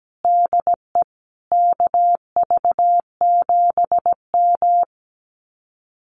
Morse code, heard as a keyed tone, about 17 wpm, 700 Hz. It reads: DE KV7M
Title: Common QSO Phrases Encore 17wpm